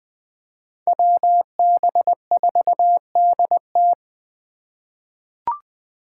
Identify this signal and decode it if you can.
Morse code, heard as a keyed tone, about 21 wpm, 700 Hz. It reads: WB4DT
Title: USA Call Signs - 5 Characters - 3x Character Spacing - 20wpm